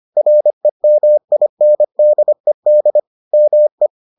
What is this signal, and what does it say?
Morse code, heard as a keyed tone, about 25 wpm, 600 Hz. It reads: REMINDED ME